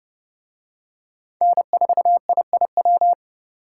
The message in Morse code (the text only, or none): N4IIW